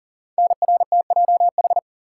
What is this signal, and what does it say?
Morse code, heard as a keyed tone, about 42 wpm, 690 Hz. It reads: NRTJH
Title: Code Groups - 5 Letters - 40wpm